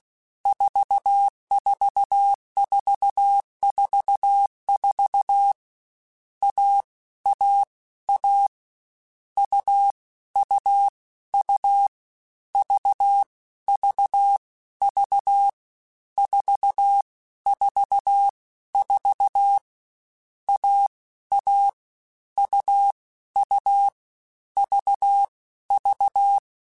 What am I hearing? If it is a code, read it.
Morse code, heard as a keyed tone, about 16 wpm, 800 Hz. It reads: 44444 A A A U U U V V V 4 4 4 A A U U V V